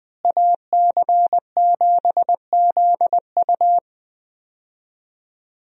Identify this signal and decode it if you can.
Morse code, heard as a keyed tone, about 20 wpm, 700 Hz. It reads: AC7ZU
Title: USA Call Signs - 5 Characters - Repeated 3x - 20wpm